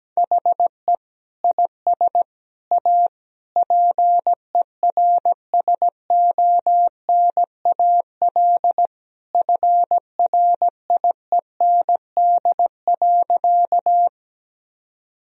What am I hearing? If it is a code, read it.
Morse code, heard as a keyed tone, about 17 wpm, 700 Hz. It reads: HE IS A PERSONAL FRIEND.
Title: Sentences from Top 1000 Words - Review Part 4 - 17wpm